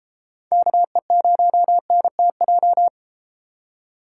Morse code, heard as a keyed tone, about 34 wpm, 700 Hz. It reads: KE0NTJ